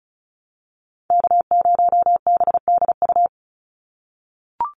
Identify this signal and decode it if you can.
Morse code, heard as a keyed tone, about 34 wpm, 700 Hz. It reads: K0BDU
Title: USA Call Signs 35wpm